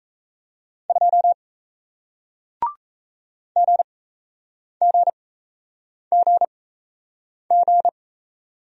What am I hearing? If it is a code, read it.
Morse code, heard as a keyed tone, about 37 wpm, spaced as for 8 wpm, 700 Hz. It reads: 2 GGGG